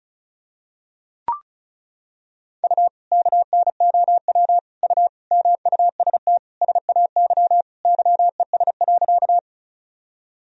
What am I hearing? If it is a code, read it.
Morse code, heard as a keyed tone, about 36 wpm, 700 Hz. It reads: U KNOW U MUST SAY YES.
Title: Sentences from Top 500 Words and Abbreviations 35wpm